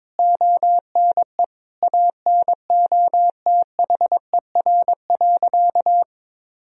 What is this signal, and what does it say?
Morse code, heard as a keyed tone, about 22 wpm, 700 Hz. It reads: ONE ANOTHER.
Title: Sentences from Top 1000 Words - Review Part 1 - 22wpm